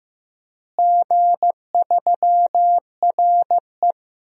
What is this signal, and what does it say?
Morse code, heard as a keyed tone, about 15 wpm, 700 Hz. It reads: G3RE
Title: DXCC-100 Call Signs - 4 Characters - Repeated 3x - 15wpm